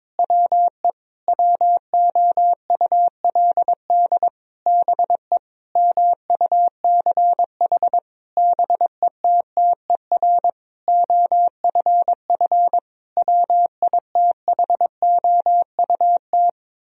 Morse code, heard as a keyed tone, about 22 wpm, 700 Hz. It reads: WE WOULD BE MUCH BETTER OFF WITHOUT
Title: Sentences from Top 1000 Words - Review Part 3 - 22wpm